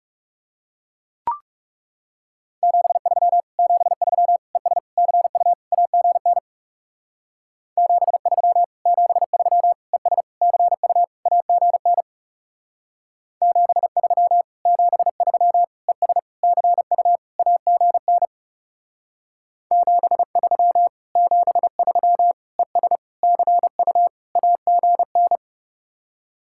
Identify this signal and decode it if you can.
Morse code, heard as a keyed tone, about 34 wpm, 700 Hz. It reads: 73 73 ES CU AGN 73 73 ES CU AGN 73 73 ES CU AGN 73 73 ES CU AGN
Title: Speed-Racing Common QSO Phrases - 30wpm